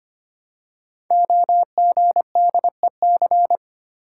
Morse code, heard as a keyed tone, about 25 wpm, 700 Hz. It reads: OGDEC
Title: Code Groups - 5 Letters - 25wpm